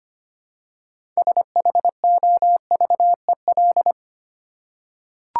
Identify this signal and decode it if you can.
Morse code, heard as a keyed tone, about 25 wpm, 700 Hz. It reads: SHOVEL
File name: practice / 25wpm / Easy Words with Hints - 25wpm